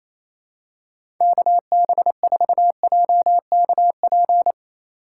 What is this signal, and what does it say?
Morse code, heard as a keyed tone, about 28 wpm, 700 Hz. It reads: KB4JKP
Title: USA Call Signs - 6 Characters - 3x Character Spacing - 28wpm